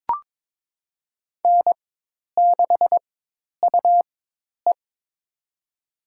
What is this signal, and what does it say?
Morse code, heard as a keyed tone, about 21 wpm, spaced as for 10 wpm, 700 Hz. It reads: N6UE
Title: USA Call Signs - 4 Characters - 4x Character Spacing - 22wpm